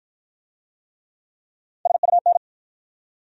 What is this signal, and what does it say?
Morse code, heard as a keyed tone, about 53 wpm, 700 Hz. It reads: SUN